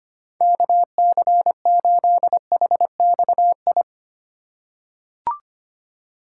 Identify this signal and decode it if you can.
Morse code, heard as a keyed tone, about 25 wpm, 700 Hz. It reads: KC8HXI
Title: USA Call Signs - 3x Character Spacing - 25wpm